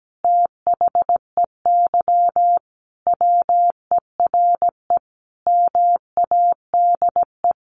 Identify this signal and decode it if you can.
Morse code, heard as a keyed tone, about 17 wpm, 700 Hz. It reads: THEY WERE MADE
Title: Sentences from Top 700 Words 17wpm